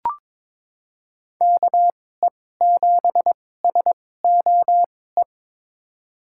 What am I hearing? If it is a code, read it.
Morse code, heard as a keyed tone, about 22 wpm, spaced as for 16 wpm, 700 Hz. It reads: KE7SOE
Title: USA Call Signs - 6 Characters - 2x Character Spacing - 22wpm